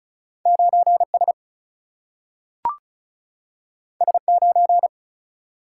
Morse code, heard as a keyed tone, about 36 wpm, 700 Hz. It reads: 9S S9
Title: Single-Letter Single-Number 35wpm